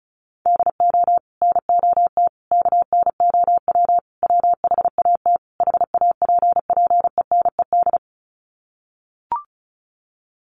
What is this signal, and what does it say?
Morse code, heard as a keyed tone, about 34 wpm, 700 Hz. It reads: DO NOT KNOW WHAT HAPPENED
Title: Sets of 5 Words Encore2 35wpm